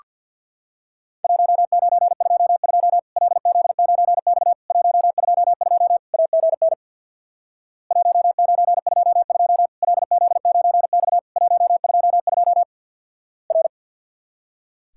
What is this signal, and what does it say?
Morse code, heard as a keyed tone, about 48 wpm, 700 Hz. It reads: 1922 LZ9X 122 AGN 1922 LZ9X 122 R
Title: Worked All Europe DX Contest - QTC Exchange with Repeats - 50wpm